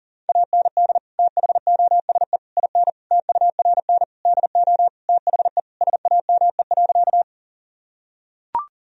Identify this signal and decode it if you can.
Morse code, heard as a keyed tone, about 40 wpm, 700 Hz. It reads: AND THOSE IN TURN DO THE SAME.
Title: Sentences from Top 1000 Words - Review Part 1 - 40wpm